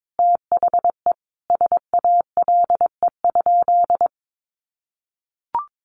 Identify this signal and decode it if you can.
Morse code, heard as a keyed tone, about 22 wpm, 700 Hz. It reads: THE SALE?
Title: Sentences from Top 1400 Words 22wpm